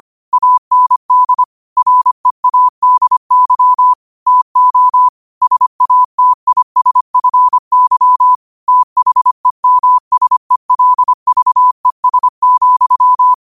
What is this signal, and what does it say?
Morse code, heard as a keyed tone, about 25 wpm, 1000 Hz. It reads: AND READY TO SATISFY THEMSELVES,
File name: qotdfiles / 25wpm